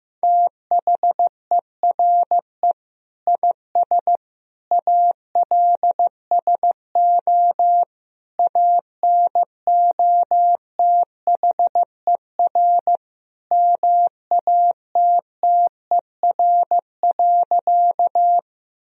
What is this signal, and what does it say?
Morse code, heard as a keyed tone, about 15 wpm, 700 Hz. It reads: THERE IS ALSO ANOTHER MATTER.
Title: Sentences from Top 1000 Words - Review Part 1 - 15wpm